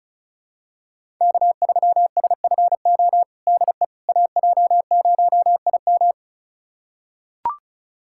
Morse code, heard as a keyed tone, about 36 wpm, 700 Hz. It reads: K3SFO DE AJ0IM
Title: USA Callsign DE Callsign - 5 Characters - 35wpm